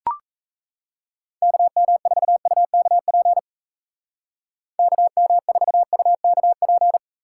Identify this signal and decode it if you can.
Morse code, heard as a keyed tone, about 35 wpm, 700 Hz. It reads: KM4UKP KM4UKP
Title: Speed Racing - USA 2x3 Call Signs 28wpm